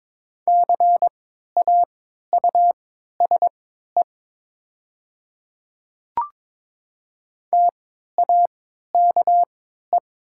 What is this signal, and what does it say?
Morse code, heard as a keyed tone, about 22 wpm, spaced as for 12 wpm, 700 Hz. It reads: CAUSE TAKE
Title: 3x Character Spacing - Top 200 Words - 22wpm